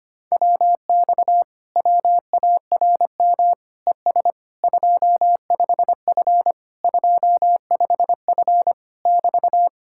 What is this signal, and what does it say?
Morse code, heard as a keyed tone, about 29 wpm, 700 Hz. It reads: WX WARM ES 25F 25F <BT>
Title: Speed-Racing Common QSO Phrases Encore - 25wpm